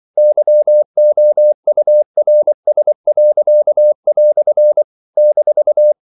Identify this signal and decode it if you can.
Morse code, heard as a keyed tone, about 24 wpm, 600 Hz. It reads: YOURS." -
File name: CW / Practice / EnglishQuotations0424